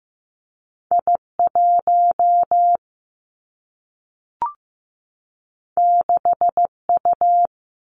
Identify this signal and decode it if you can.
Morse code, heard as a keyed tone, about 15 wpm, 700 Hz. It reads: I 1 6 U
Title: Two Challenging Letter-Number Combinations 15wpm